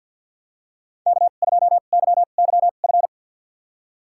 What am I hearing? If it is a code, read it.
Morse code, heard as a keyed tone, about 53 wpm, spaced as for 37 wpm, 700 Hz. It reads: K2YYF
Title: USA Call Signs - 5 Characters - 2x Character Spacing - 50wpm